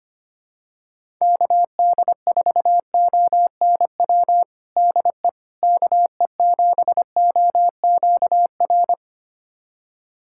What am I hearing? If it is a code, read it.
Morse code, heard as a keyed tone, about 25 wpm, 700 Hz. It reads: KD4ONW DE KE7OQR
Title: USA Callsign DE Callsign - 6 Characters - 25wpm